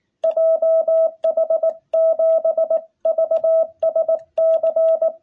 Morse code, heard as a keyed tone, about 18 wpm, 660 Hz. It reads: JH7VSC
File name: jh8cbh / 2021cw